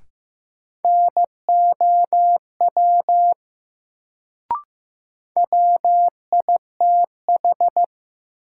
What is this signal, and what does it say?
Morse code, heard as a keyed tone, about 15 wpm, 700 Hz. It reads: NOW WITH
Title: Sentences from Top 100 Words - New Word Drill - 15wpm